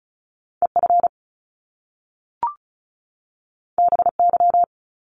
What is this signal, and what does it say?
Morse code, heard as a keyed tone, about 36 wpm, 700 Hz. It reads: EF BY
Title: Top Two Letter Combinations in Words 35wpm